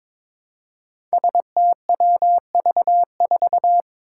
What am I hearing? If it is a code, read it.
Morse code, heard as a keyed tone, about 22 wpm, 700 Hz. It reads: STWV4
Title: Five Challenging Letter-Number Combinations 22wpm